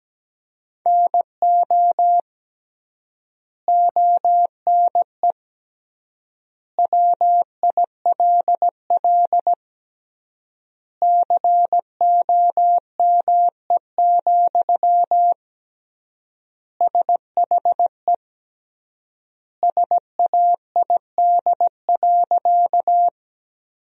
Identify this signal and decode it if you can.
Morse code, heard as a keyed tone, about 17 wpm, 700 Hz. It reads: NO ONE WILL COME, SHE SAID.
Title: 3x Word-Spacing - Sentences from Top 100 Words - 17wpm